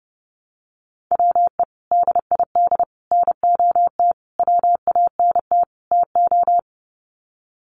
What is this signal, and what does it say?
Morse code, heard as a keyed tone, about 30 wpm, 700 Hz. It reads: WE DID NOT WANT TO
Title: Sets of 5 Words Encore 30wpm